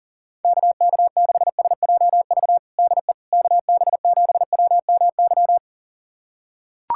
Morse code, heard as a keyed tone, about 40 wpm, 700 Hz. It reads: KK6SJV DE KB7WMY
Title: USA Callsign DE Callsign - 6 Characters - 40wpm